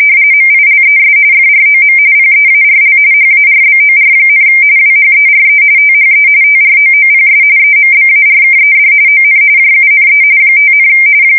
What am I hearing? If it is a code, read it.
RTTY, about 45.45 baud. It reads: MILES FROM PEARL HARBOR, WHERE THE TROOPS EMBARKED ON JUNE 5,
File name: fsk170 / USS-New-Jersey